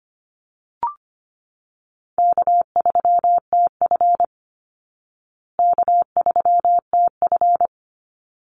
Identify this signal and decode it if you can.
Morse code, heard as a keyed tone, about 25 wpm, 700 Hz. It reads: K3TF K3TF
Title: Easy Call Signs Repeated 2x - 25wpm